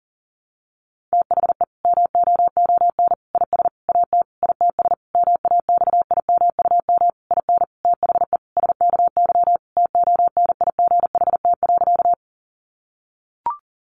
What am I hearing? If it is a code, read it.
Morse code, heard as a keyed tone, about 40 wpm, 700 Hz. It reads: THE MOON IS AT ITS MAXIMUM IN THE SKY TONIGHT.
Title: Sentences from Top 1300 Words 40wpm